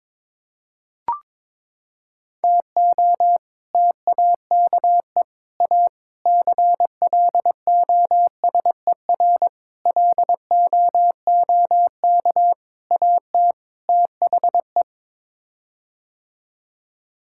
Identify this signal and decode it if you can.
Morse code, heard as a keyed tone, about 22 wpm, 700 Hz. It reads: TO TAKE A CLOSER LOOK AT THE
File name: practice / 22wpm / Sets of 7 Words - 22wpm